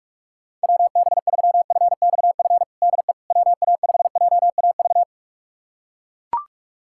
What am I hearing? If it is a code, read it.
Morse code, heard as a keyed tone, about 41 wpm, 690 Hz. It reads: WB3FXF DE WA5JAV
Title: USA Callsign DE Callsign 45wpm